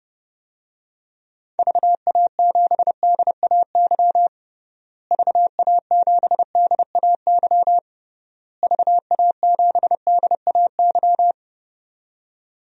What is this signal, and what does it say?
Morse code, heard as a keyed tone, about 30 wpm, 700 Hz. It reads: VA7DAY VA7DAY VA7DAY
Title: DXCC-100 Call Signs - 6 Characters - Repeated 3x - 30wpm